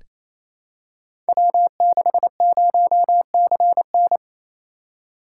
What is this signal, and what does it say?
Morse code, heard as a keyed tone, about 29 wpm, 700 Hz. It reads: W60CN
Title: Code Groups - 5 Letter:Numbers - 28wpm